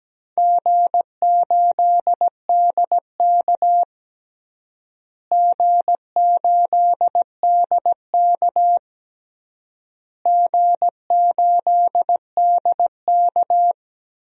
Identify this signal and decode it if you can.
Morse code, heard as a keyed tone, about 17 wpm, 700 Hz. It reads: G8DK G8DK G8DK
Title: DXCC-100 Call Signs - 4 Characters - Repeated 3x - 17wpm